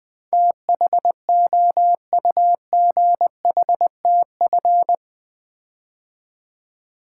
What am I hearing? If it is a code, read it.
Morse code, heard as a keyed tone, about 20 wpm, 700 Hz. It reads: THOUGHTF
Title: Build A Word - 4 Letters At A Time - 10-Letter Words - 20wpm